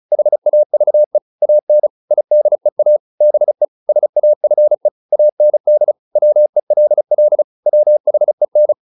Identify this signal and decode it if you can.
Morse code, heard as a keyed tone, about 34 wpm, 600 Hz. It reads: HAVE AN IDEA BE SAFE AND WELL WHEN